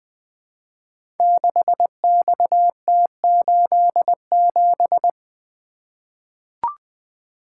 Morse code, heard as a keyed tone, about 20 wpm, 700 Hz. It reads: 6XT87
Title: Code Groups - 5 Letter:Numbers - 20wpm